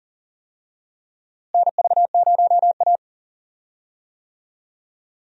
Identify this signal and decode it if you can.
Morse code, heard as a keyed tone, about 40 wpm, 700 Hz. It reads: NV0A
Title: USA Call Signs - 4 Characters - Repeated 2x - 40wpm